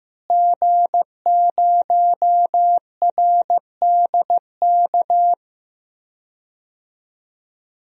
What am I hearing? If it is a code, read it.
Morse code, heard as a keyed tone, about 15 wpm, 700 Hz. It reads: G0RDK
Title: DXCC-100 Call Signs - 5 Characters - Repeated 2x - 15wpm